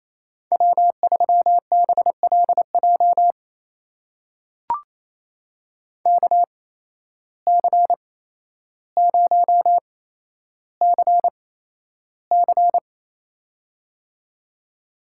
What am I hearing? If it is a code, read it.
Morse code, heard as a keyed tone, about 28 wpm, spaced as for 12 wpm, 700 Hz. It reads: W3BLJ KC0CC
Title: USA Call Signs - 5 Characters - 8x Character Spacing - 28wpm